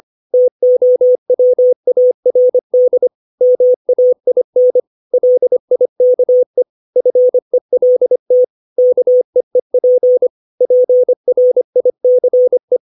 Morse code, heard as a keyed tone, about 25 wpm, 500 Hz. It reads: TOWARD MAIN LIKE FELT KEEP PRICE
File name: 632 words / 632 Words 25WPM